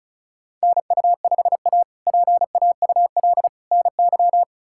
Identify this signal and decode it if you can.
Morse code, heard as a keyed tone, about 36 wpm, 700 Hz. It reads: NU5A PAUL NY
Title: K1USN SST QSO Practice - 35wpm